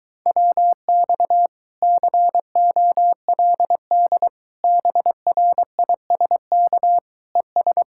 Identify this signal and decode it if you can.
Morse code, heard as a keyed tone, about 23 wpm, 700 Hz. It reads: WX COLD BRISK ES